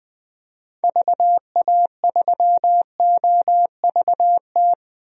Morse code, heard as a keyed tone, about 20 wpm, 700 Hz. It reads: VA3OVT
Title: DXCC-100 Call Signs - 6 Characters - Repeated 3x - 20wpm